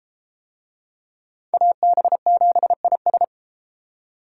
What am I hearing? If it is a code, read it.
Morse code, heard as a keyed tone, about 29 wpm, 700 Hz. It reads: AB7IS